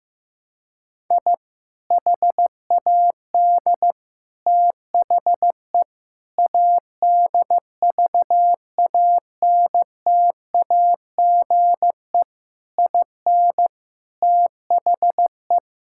Morse code, heard as a keyed tone, about 15 wpm, 700 Hz. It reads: I HAD THE ADVANTAGE IN THE
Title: Sentences from Top 1000 Words - Review Part 6 - 15wpm